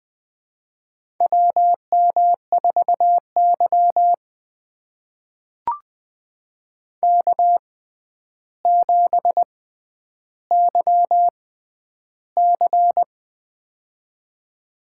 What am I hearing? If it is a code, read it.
Morse code, heard as a keyed tone, about 20 wpm, spaced as for 10 wpm, 700 Hz. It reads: WM4Y K7YC
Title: USA Call Signs - 4 Characters - 6x Character Spacing - 20wpm